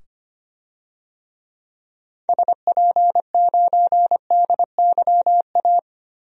Morse code, heard as a keyed tone, about 25 wpm, 700 Hz. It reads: SP9DYA
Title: DXCC-100 Call Signs - 6 Characters - 25wpm